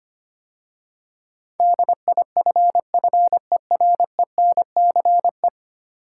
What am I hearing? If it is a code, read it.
Morse code, heard as a keyed tone, about 25 wpm, 700 Hz. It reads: DIFFERENCE
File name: 10 Letter Words - First 3 Letters Given - 25wpm